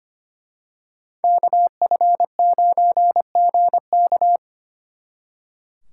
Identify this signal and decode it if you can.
Morse code, heard as a keyed tone, about 29 wpm, 700 Hz. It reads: KF9GK
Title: Speed Racing - USA 2x2 Call Signs 25wpm